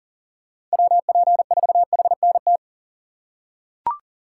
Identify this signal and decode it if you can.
Morse code, heard as a keyed tone, about 41 wpm, 700 Hz. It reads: WP4HNT
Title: USA Call Signs - 3x Character Spacing - 40wpm